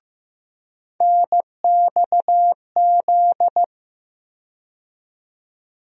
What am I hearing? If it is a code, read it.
Morse code, heard as a keyed tone, about 15 wpm, 700 Hz. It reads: NXZ
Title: USA Call Sign Postfixes - 3 Letters 15wpm